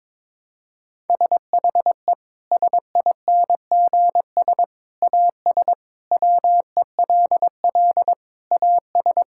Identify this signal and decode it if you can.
Morse code, heard as a keyed tone, about 22 wpm, 700 Hz. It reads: SHE SINGS AS WELL AS
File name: practice / 22wpm / Sentences with 3-Word Phrases - 22wpm